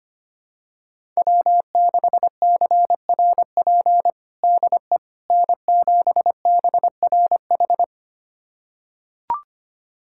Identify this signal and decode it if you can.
Morse code, heard as a keyed tone, about 25 wpm, 700 Hz. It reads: W6CRP DE N7BRH
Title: USA Callsign DE Callsign - 5 Characters - 25wpm